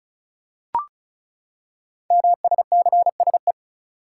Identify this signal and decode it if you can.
Morse code, heard as a keyed tone, about 36 wpm, 690 Hz. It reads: MSCSE
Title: Code Groups - 5 Letter:Numbers - 35wpm